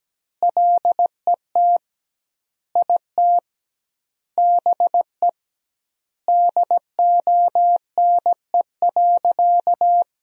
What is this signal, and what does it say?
Morse code, heard as a keyed tone, about 17 wpm, 700 Hz. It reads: LET IT BE DONE.
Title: 2x Word-Spacing - Sentences from Top 500 Words - 17wpm